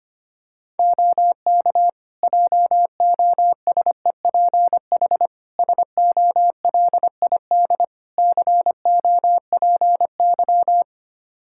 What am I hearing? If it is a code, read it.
Morse code, heard as a keyed tone, about 25 wpm, 700 Hz. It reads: OK JOSEPH SOLID COPY
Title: Practice QSO - Exchange 3 - More Info - 25wpm